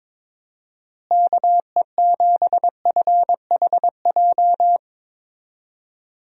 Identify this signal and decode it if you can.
Morse code, heard as a keyed tone, about 22 wpm, 700 Hz. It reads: KE7FHJ
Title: USA Call Signs - 6 Characters - Repeated 2x - 22wpm